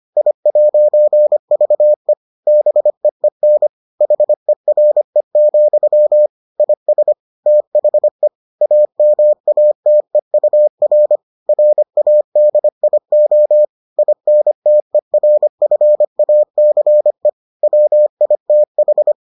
Morse code, heard as a keyed tone, about 25 wpm, 600 Hz. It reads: I'VE BEEN HERE, IS THE AMATEUR RADIO INTERFACE WITH